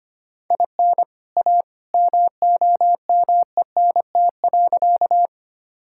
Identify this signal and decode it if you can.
Morse code, heard as a keyed tone, about 25 wpm, 700 Hz. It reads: IN A MOMENT.
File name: practice / 25wpm / Sentences with 2-Word Phrases - 25wpm